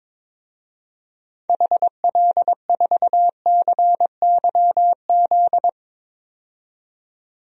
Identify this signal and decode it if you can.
Morse code, heard as a keyed tone, about 21 wpm, 700 Hz. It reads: HL4CYZ
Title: DXCC-100 Call Signs - 6 Characters - 22wpm